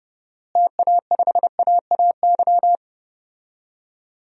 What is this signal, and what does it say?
Morse code, heard as a keyed tone, about 32 wpm, 700 Hz. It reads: TA5AAY
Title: DXCC-100 Call Signs - 6 Characters - Repeated 2x - 30wpm